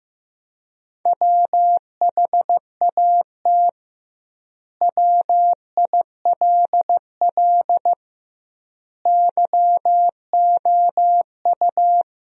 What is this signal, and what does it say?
Morse code, heard as a keyed tone, about 15 wpm, 700 Hz. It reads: WHAT WILL YOU
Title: 2x Word-Spacing - Sentences from Top 100 Words - 15wpm